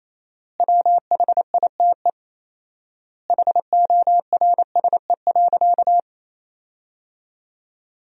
Morse code, heard as a keyed tone, about 28 wpm, 700 Hz. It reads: WHITE HORSE.
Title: 4x Word-Spacing - Sentences from Top 300 Words - 28wpm